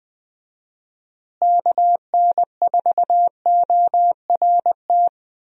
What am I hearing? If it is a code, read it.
Morse code, heard as a keyed tone, about 20 wpm, 700 Hz. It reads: KN4ORT
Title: USA Call Signs - 6 Characters - Repeated 2x - 20wpm